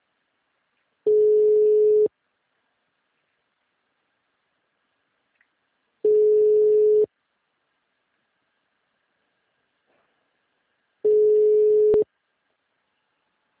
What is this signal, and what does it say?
Ringback tone (Europe, 425 Hz).
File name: out- -2017- - - .20728